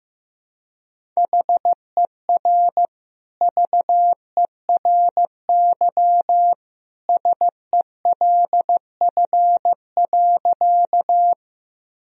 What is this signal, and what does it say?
Morse code, heard as a keyed tone, about 15 wpm, 700 Hz. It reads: HER VERY SELF.
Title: Sentences from Top 1000 Words - Review Part 1 - 15wpm